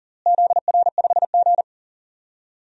Morse code, heard as a keyed tone, about 40 wpm, 700 Hz. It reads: ZR5G
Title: DXCC-100 Call Signs - 4 Characters - Repeated 2x - 40wpm